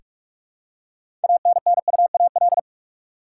Morse code, heard as a keyed tone, about 42 wpm, 700 Hz. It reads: ANNUAL
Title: Six Letter Words - 45wpm